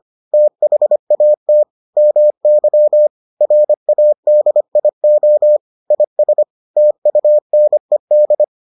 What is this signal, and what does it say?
Morse code, heard as a keyed tone, about 25 wpm, 600 Hz. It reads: THAT MY RADIO IS TUNED